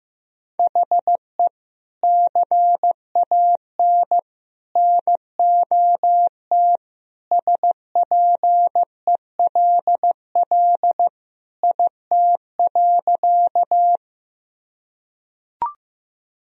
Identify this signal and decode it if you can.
Morse code, heard as a keyed tone, about 16 wpm, 700 Hz. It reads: HE CAN NOT SPELL IT.
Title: Sentences from Top 1000 Words - Review Part 1 - 15wpm